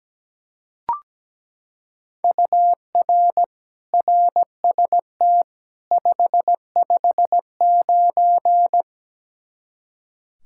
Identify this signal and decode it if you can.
Morse code, heard as a keyed tone, about 17 wpm, 700 Hz. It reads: UR RST 559
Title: Practice QSO - Exchange 1 - Beginning - 17wpm